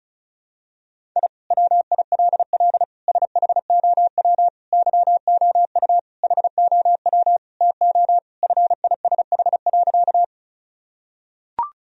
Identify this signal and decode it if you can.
Morse code, heard as a keyed tone, about 34 wpm, 700 Hz. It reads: I WILL SHOW YOU HOW TO FISH.
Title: Sentences from Top 1000 Words - Review Part 1 - 35wpm